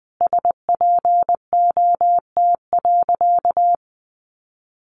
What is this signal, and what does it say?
Morse code, heard as a keyed tone, about 19 wpm, 700 Hz. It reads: SPOT.
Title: Sentences from Top 1200 Words 20wpm